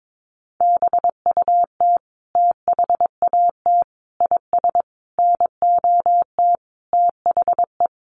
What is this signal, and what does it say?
Morse code, heard as a keyed tone, about 22 wpm, 700 Hz. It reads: BUT THAT IS NOT THE
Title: Sentences from Top 600 Words 22wpm